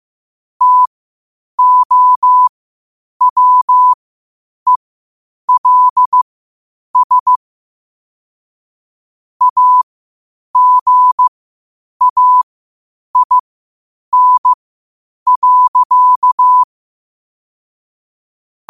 Morse code, heard as a keyed tone, about 15 wpm, spaced as for 8 wpm, 1000 Hz. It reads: TOWELS AGAIN.
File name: qotdfiles / 8wpm